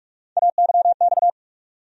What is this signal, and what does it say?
Morse code, heard as a keyed tone, about 47 wpm, 700 Hz. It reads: AYX
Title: USA Call Sign Postfixes - 3 Letters 45wpm